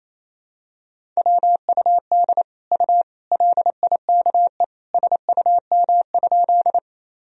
Morse code, heard as a keyed tone, about 28 wpm, 700 Hz. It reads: WUD U LIKE SUM?
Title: Sentences from Top 100 Words and Abbreviations 28wpm